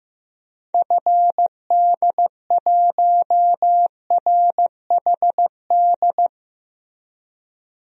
Morse code, heard as a keyed tone, about 15 wpm, 700 Hz. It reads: FD1RHD
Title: DXCC-100 Call Signs - 6 Characters - Repeated 2x - 15wpm